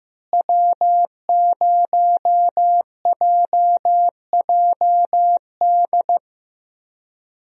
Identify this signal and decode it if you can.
Morse code, heard as a keyed tone, about 15 wpm, 700 Hz. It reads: W0JJD
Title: USA Call Signs - 5 Characters - Repeated 3x - 15wpm